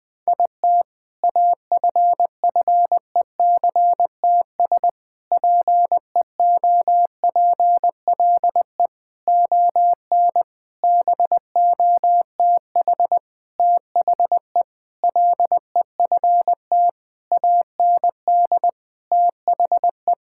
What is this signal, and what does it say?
Morse code, heard as a keyed tone, about 21 wpm, 700 Hz. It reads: IT AFFECTS PEOPLE ON BOTH THE LEFT AND THE
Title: Sentences with 7-Word Phrases - Spoken Beginning - 20wpm